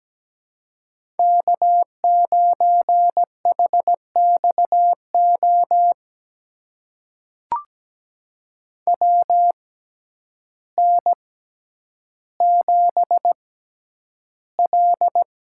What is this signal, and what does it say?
Morse code, heard as a keyed tone, about 17 wpm, spaced as for 9 wpm, 700 Hz. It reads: K9HXO WN7L
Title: USA Call Signs - 6x Character Spacing - 17wpm